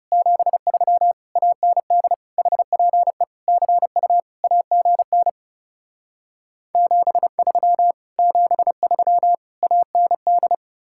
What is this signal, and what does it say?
Morse code, heard as a keyed tone, about 40 wpm, 700 Hz. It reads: 73 AND HPE CU AGN 73 73 AND